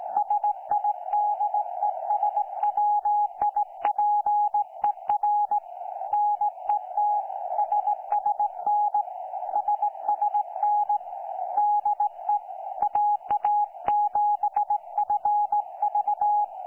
Morse code, heard as a keyed tone, about 17 wpm, 830 Hz. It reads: SIDESWIPER NET SSN SSN DE AA7FV
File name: 09 38Z 18088.0kHz k3fef darrel